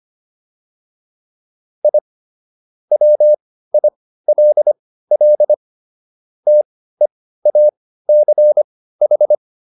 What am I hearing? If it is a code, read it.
Morse code, heard as a keyed tone, about 25 wpm, spaced as for 15 wpm, 600 Hz. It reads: I WILL TEACH